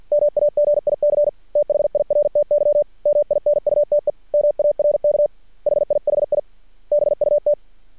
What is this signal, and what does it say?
Morse code, heard as a keyed tone, about 50 wpm, 590 Hz. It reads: MAGIC THIRTY MINUTE MARK HIHI BUT
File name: archived / CRSnet